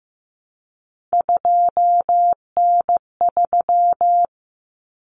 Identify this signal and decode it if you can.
Morse code, heard as a keyed tone, about 15 wpm, 700 Hz. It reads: 2N3
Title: Common Cut Numbers - 3-Digit 15wpm